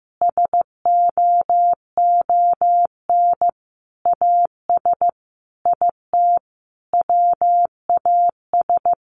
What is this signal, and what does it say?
Morse code, heard as a keyed tone, about 15 wpm, 700 Hz. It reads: SOON AS IT WAS
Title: Sentences from Top 600 Words 15wpm